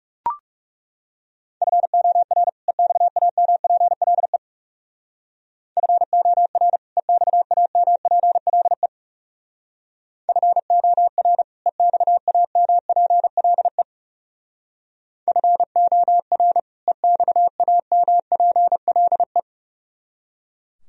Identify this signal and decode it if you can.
Morse code, heard as a keyed tone, about 38 wpm, 700 Hz. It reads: FOR EXAMPLE FOR EXAMPLE FOR EXAMPLE FOR EXAMPLE